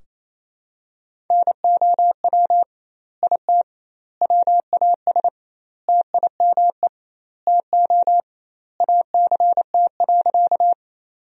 Morse code, heard as a keyed tone, about 28 wpm, 700 Hz. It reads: NOW IT WAS TIME TO ACT.
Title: 2x Word-Spacing - Sentences from Top 200 Words - 28wpm